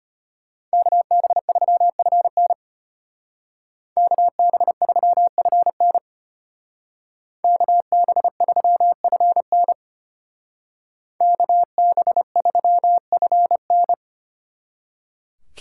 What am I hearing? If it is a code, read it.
Morse code, heard as a keyed tone, about 30 wpm, 700 Hz. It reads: KB3FN KB3FN KB3FN KB3FN
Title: Speed Racing - USA 2x2 Call Signs 25wpm